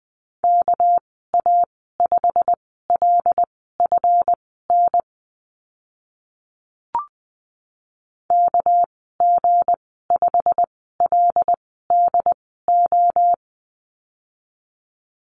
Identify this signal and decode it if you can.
Morse code, heard as a keyed tone, about 20 wpm, spaced as for 15 wpm, 700 Hz. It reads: KA5LFN KG5LDO
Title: 200 Lesson 32g - Letter k - Call Signs 20wpm